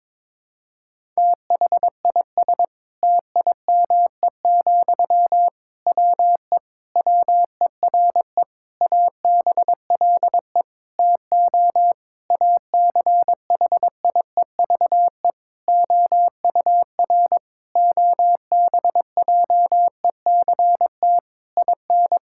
Morse code, heard as a keyed tone, about 22 wpm, 700 Hz. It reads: THIS TIME, WE WERE ABLE TO ACHIEVE OUR OBJECT IN
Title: Sentences from Top 1000 Words - Review Part 6 - 22wpm